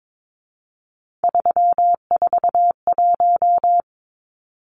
Morse code, heard as a keyed tone, about 22 wpm, 700 Hz. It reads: 341
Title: Numbers - 3-Digit 22wpm